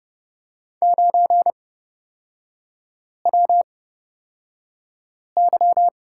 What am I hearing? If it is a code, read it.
Morse code, heard as a keyed tone, about 30 wpm, spaced as for 5 wpm, 700 Hz. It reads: 9WY
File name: Lesson 36e - BT - Three Characters 30wpm